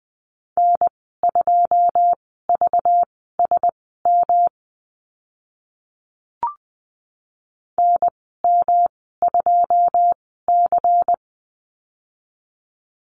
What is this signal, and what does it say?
Morse code, heard as a keyed tone, about 20 wpm, spaced as for 14 wpm, 700 Hz. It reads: N2VSM NM2C
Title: 213 Lesson 34g - Number 8 - Call Signs 20wpm